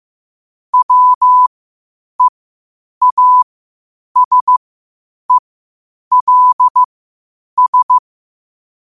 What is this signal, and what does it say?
Morse code, heard as a keyed tone, about 15 wpm, spaced as for 8 wpm, 1000 Hz. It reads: WEASELS
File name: qotdfiles / 8wpm